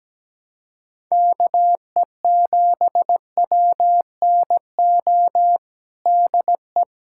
Morse code, heard as a keyed tone, about 17 wpm, 700 Hz. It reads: KE7WNO DE
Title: USA Callsign DE Callsign - 6 Characters - 17wpm